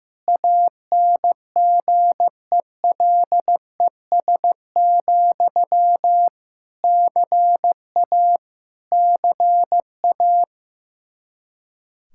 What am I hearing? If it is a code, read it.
Morse code, heard as a keyed tone, about 15 wpm, 700 Hz. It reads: ANGELES, CA CA
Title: Practice QSO - Exchange 1 - Beginning - 15wpm